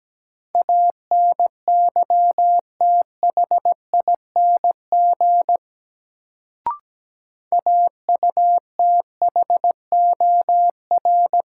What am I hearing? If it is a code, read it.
Morse code, heard as a keyed tone, about 17 wpm, 700 Hz. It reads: ANYTHING AUTHOR
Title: Sets of 4 Words - New Word Drill - 17wpm